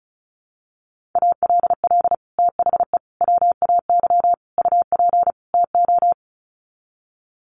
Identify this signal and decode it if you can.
Morse code, heard as a keyed tone, about 34 wpm, 700 Hz. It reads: ALL THE WAY UP TO
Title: Sets of 5 Words Encore 35wpm